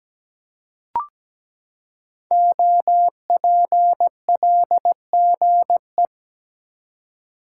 Morse code, heard as a keyed tone, about 17 wpm, 700 Hz. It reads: OPLGE
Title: Code Groups - 5 Letters - 17wpm